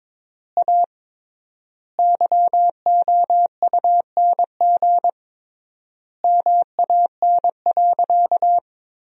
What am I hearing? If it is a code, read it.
Morse code, heard as a keyed tone, about 22 wpm, 700 Hz. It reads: A YOUNG MAN.
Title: 3x Word-Spacing - Sentences from Top 500 Words - 22wpm